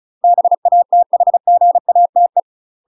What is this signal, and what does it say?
Morse code, heard as a keyed tone, about 36 wpm, 700 Hz. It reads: BATHGATE
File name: UK MASTER QSOs TRAINING / 35WPM / UK MASTER QSOs 0011